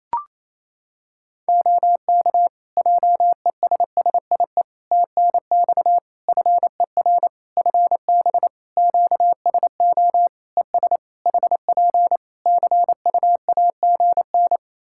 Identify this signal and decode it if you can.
Morse code, heard as a keyed tone, about 27 wpm, 700 Hz. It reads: OK JESSIE TNX FER FB QSO ES HP CUAGN
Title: Practice QSO - Exchange 4 - Ending - 28wpm